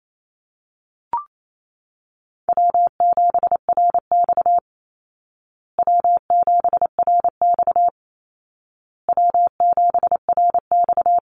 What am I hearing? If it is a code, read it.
Morse code, heard as a keyed tone, about 28 wpm, 700 Hz. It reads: W7RX W7RX W7RX
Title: Easy Call Signs Repeated 3x - 28wpm